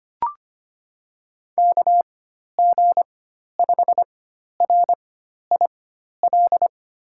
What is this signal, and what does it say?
Morse code, heard as a keyed tone, about 25 wpm, spaced as for 12 wpm, 700 Hz. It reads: KG5RIL
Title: USA Call Signs - 6 Characters - 4x Character Spacing - 25wpm